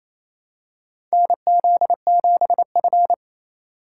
Morse code, heard as a keyed tone, about 28 wpm, 700 Hz. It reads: NZ7F
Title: USA Call Signs - 3x Character Spacing - 28wpm